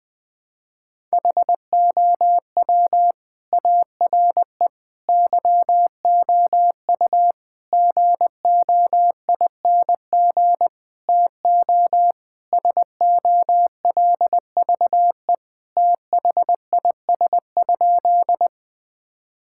Morse code, heard as a keyed tone, about 21 wpm, 700 Hz. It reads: HOW ARE YOU GOING TO SOLVE THIS?
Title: Sentences with 5-Word Phrases - Spoken Beginning - 20wpm